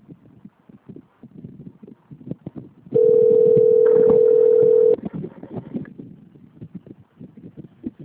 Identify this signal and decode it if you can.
Ringback tone (North America), 440 + 480 Hz.